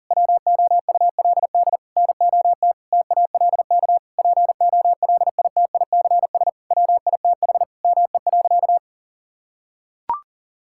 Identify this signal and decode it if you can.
Morse code, heard as a keyed tone, about 40 wpm, 700 Hz. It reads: WOULD NOT TALK POLITICS WITH ME.
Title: Sentences from Top 1000 Words - Review Part 6 - 40wpm